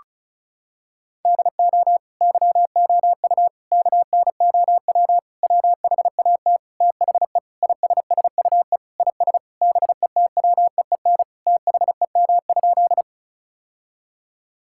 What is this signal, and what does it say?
Morse code, heard as a keyed tone, about 36 wpm, 700 Hz. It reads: DO YOU KNOW WHAT THE ISSUE IS BETWEEN THEM?
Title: Sentences from Top 1000 Words - Review Part 6 - 35wpm